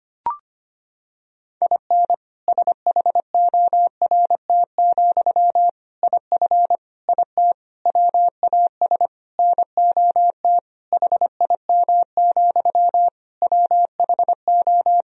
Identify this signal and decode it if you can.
Morse code, heard as a keyed tone, about 25 wpm, 700 Hz. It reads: IN SHORT, IF IT WAS NOT HIM, WHO
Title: Sentences from Top 1000 Words - Review Part 2 - 25wpm